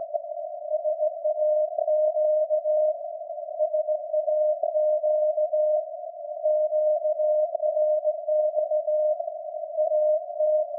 Morse code, heard as a keyed tone, about 17 wpm, 610 Hz. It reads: SAQ SAQ QRX AT
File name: iz7slz / VLF / SAQ jun 2019